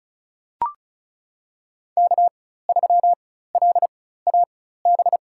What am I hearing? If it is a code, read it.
Morse code, heard as a keyed tone, about 36 wpm, spaced as for 16 wpm, 700 Hz. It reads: K3LAB
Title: USA Call Signs - 5 Characters - 4x Character Spacing - 35wpm